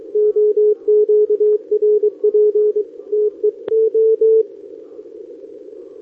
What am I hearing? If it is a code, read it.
Morse code, heard as a keyed tone, about 23 wpm, 410 Hz. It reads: OQRP TE O
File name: OK QRP 2026 / OKQRP2026